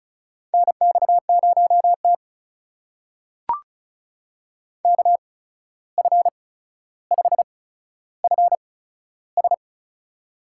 Morse code, heard as a keyed tone, about 36 wpm, spaced as for 14 wpm, 700 Hz. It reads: NX0T KF5FS
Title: USA Call Signs - 8x Character Spacing - 35wpm